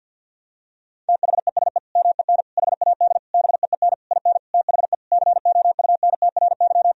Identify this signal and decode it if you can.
Morse code, heard as a keyed tone, about 51 wpm, 700 Hz. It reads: THESE MEN HAD BEEN IN THE COUNTRY
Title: Sentences from Top 1000 Words - Review Part 1 - 50wpm